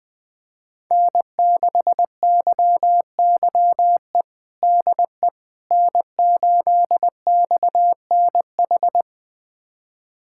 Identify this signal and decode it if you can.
Morse code, heard as a keyed tone, about 21 wpm, 700 Hz. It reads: N6YYE DE N8XNH
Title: USA Callsign DE Callsign - 5 Characters - 20wpm